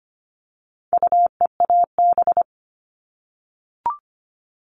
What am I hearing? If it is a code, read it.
Morse code, heard as a keyed tone, about 25 wpm, 700 Hz. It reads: UEAB
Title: All Cut Numbers - 4-Digit 25wpm